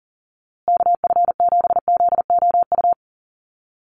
Morse code, heard as a keyed tone, about 39 wpm, 700 Hz. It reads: KF7ZOU
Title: USA Call Signs Repeated 3x 40wpm